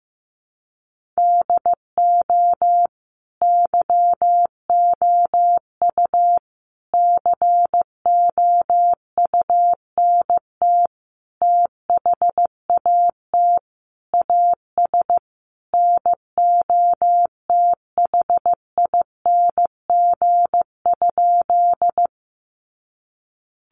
Morse code, heard as a keyed tone, about 15 wpm, 700 Hz. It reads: DO YOU COUNT THAT AS NOTHING?
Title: Sentences from Top 1500 Words 15wpm